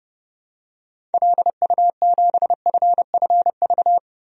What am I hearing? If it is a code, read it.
Morse code, heard as a keyed tone, about 30 wpm, 700 Hz. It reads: LU7FFV
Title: DXCC-100 Call Signs - 6 Characters - 30wpm